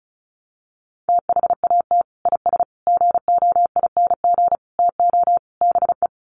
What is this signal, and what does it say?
Morse code, heard as a keyed tone, about 34 wpm, 700 Hz. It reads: THAT IS GOING TO BE
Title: Sets of 5 Words Encore2 35wpm